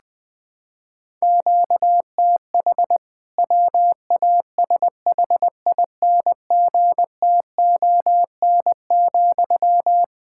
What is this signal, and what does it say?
Morse code, heard as a keyed tone, about 20 wpm, 700 Hz. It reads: QTH WASHINGTON,
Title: K1USN SST QSO Practice 20wpm